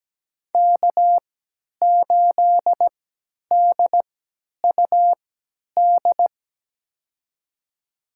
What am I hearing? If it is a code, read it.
Morse code, heard as a keyed tone, about 17 wpm, spaced as for 10 wpm, 700 Hz. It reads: K8DUD
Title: USA Call Signs - 5 Characters - 3x Character Spacing - 17wpm